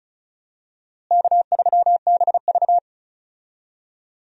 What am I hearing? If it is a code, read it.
Morse code, heard as a keyed tone, about 36 wpm, 700 Hz. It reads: K3BV
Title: USA Call Signs - 4 Characters - Repeated 3x - 35wpm